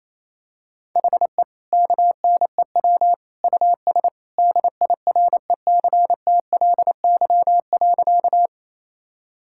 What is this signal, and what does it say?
Morse code, heard as a keyed tone, about 28 wpm, 700 Hz. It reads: HE KNEW US DIRECTLY.
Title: Sentences from Top 1000 Words - Review Part 5 - 28wpm